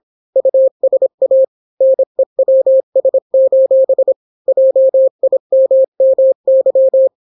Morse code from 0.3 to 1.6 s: USA